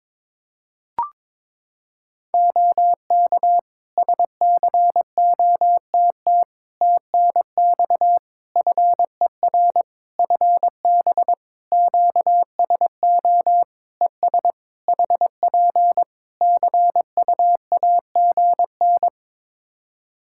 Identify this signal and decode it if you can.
Morse code, heard as a keyed tone, about 22 wpm, 700 Hz. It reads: OK SCOTT TNX FER FB QSO ES HP CUAGN
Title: Practice QSO - Exchange 4 - Ending - 22wpm